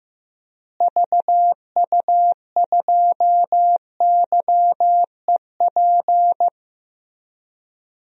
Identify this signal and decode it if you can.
Morse code, heard as a keyed tone, about 16 wpm, 700 Hz. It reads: VU2YEP
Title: DXCC-100 Call Signs - 6 Characters - Repeated 2x - 15wpm